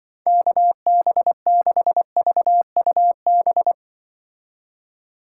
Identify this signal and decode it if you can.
Morse code, heard as a keyed tone, about 24 wpm, 700 Hz. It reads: KB6VUB
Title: Speed Racing - USA 2x3 Call Signs 20wpm